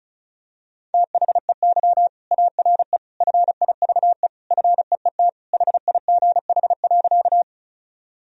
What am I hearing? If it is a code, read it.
Morse code, heard as a keyed tone, about 34 wpm, 700 Hz. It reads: THEY ARE FIVE FEET HIGH.
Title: Sentences from Top 1000 Words - Review Part 2 - 35wpm